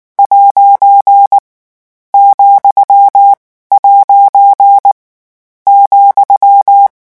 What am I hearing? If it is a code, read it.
Morse code, heard as a keyed tone, about 19 wpm, 800 Hz. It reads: ' , ' ,
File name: morse mp3 / lecon14-vitesse 18